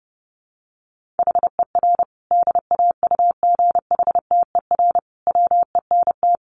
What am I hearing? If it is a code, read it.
Morse code, heard as a keyed tone, about 30 wpm, 700 Hz. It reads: HER DAUGHTER WENT
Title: Sentences from Top 1500 Words 30wpm